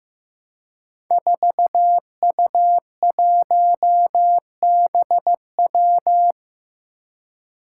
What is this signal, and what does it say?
Morse code, heard as a keyed tone, about 15 wpm, 700 Hz. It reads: 4U1BW
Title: Five Challenging Letter-Number Combinations 15wpm